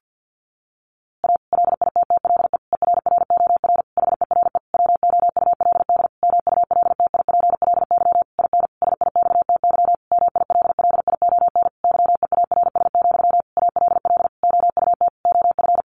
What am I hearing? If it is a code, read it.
Morse code, heard as a keyed tone, about 50 wpm, 700 Hz. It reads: A LITTLE ERROR HERE WOULD MULTIPLY IN SIXTY MILLION YEARS, ALL OUT OF
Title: A Sound Of Thunder 50wpm